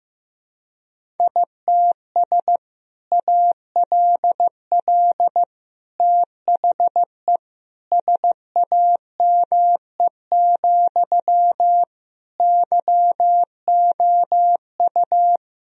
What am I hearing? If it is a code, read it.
Morse code, heard as a keyed tone, about 16 wpm, 700 Hz. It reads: ITS ALL THE SAME, YOU
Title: Sentences from Top 1000 Words - Review Part 3 - 15wpm